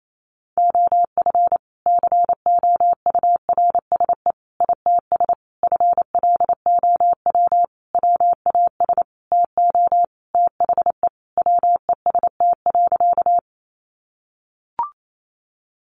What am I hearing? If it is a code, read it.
Morse code, heard as a keyed tone, about 28 wpm, 700 Hz. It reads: OF COURSE ITS FLOW WAS TO THE WEST.
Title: Sentences from Top 1200 Words 28wpm